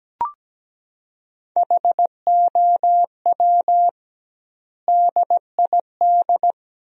Morse code, heard as a keyed tone, about 17 wpm, 700 Hz. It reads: HOW DID
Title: 2x Word-Spacing - Sentences from Top 200 Words - 17wpm